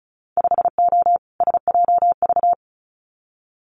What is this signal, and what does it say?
Morse code, heard as a keyed tone, about 36 wpm, 700 Hz. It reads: 5O SJV
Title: Winter Field Day Exchanges Encore 35wpm